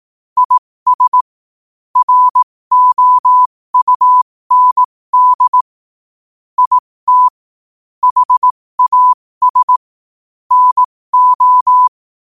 Morse code, heard as a keyed tone, about 18 wpm, spaced as for 15 wpm, 1000 Hz. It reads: IS ROUND IT HAS NO
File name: 15wpm